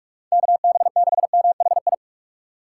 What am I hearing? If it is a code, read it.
Morse code, heard as a keyed tone, about 54 wpm, 700 Hz. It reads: KB6MHI
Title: Speed Racing - USA 2x3 Call Signs 45wpm